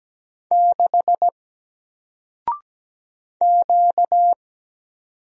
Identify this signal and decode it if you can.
Morse code, heard as a keyed tone, about 17 wpm, 700 Hz. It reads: 6 Q
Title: Single Letter-Number Alt 17wpm